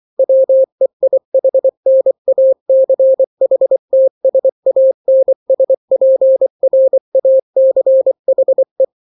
Morse code, heard as a keyed tone, about 24 wpm, 530 Hz. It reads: WEIHNACHTSANSPRACHE
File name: daily / 23wpm